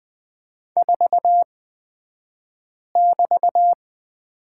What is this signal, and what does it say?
Morse code, heard as a keyed tone, about 20 wpm, spaced as for 5 wpm, 700 Hz. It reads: 4<BT>
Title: 223 Lesson 36d - BT - Two Characters 20wpm